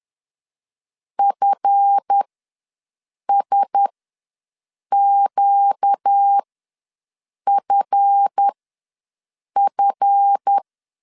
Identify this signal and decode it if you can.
Morse code, heard as a keyed tone, about 10 wpm, spaced as for 6 wpm, 800 Hz. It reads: FSQFF
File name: LEZIONI / 7 GF9Z / 7 GF ws 10wpm 3918